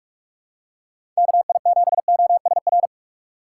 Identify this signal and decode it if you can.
Morse code, heard as a keyed tone, about 41 wpm, 700 Hz. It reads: KI7OSR
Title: USA Call Signs - 6 Characters - 2x Character Spacing - 45wpm